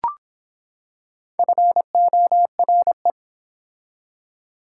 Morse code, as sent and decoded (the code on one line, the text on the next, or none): ..-. --- .-. .
FORE